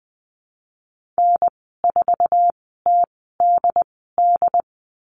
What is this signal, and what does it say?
Morse code, heard as a keyed tone, about 20 wpm, spaced as for 15 wpm, 700 Hz. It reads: N4TDD
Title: 076 Lesson 13g - Letter L - Call Signs 20wpm